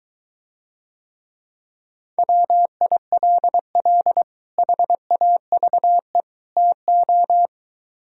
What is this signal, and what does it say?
Morse code, heard as a keyed tone, about 23 wpm, 700 Hz. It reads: WILL HAVE TO